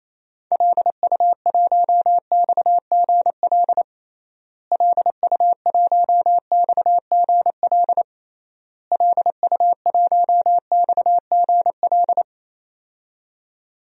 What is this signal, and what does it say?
Morse code, heard as a keyed tone, about 28 wpm, 700 Hz. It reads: LU1XGL LU1XGL LU1XGL
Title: DXCC-100 Call Signs - 6 Characters - Repeated 3x - 28wpm